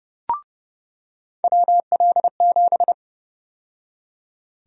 Morse code, heard as a keyed tone, about 31 wpm, 700 Hz. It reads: WL7
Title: USA Call Sign Prefixes - 30wpm